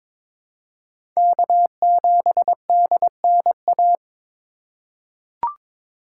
Morse code, heard as a keyed tone, about 22 wpm, 700 Hz. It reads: K7DNA
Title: USA Call Signs - 5 Characters - 4x Character Spacing - 22wpm